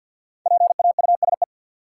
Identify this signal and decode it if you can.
Morse code, heard as a keyed tone, about 50 wpm, 700 Hz. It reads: PAUSE
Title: Five Letter Words - 50wpm